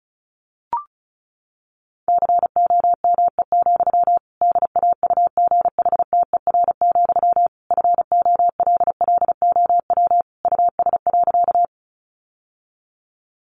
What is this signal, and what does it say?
Morse code, heard as a keyed tone, about 34 wpm, 700 Hz. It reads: COME, DAUGHTER, FOLLOW US.
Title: Sentences from Top 1500 Words 35wpm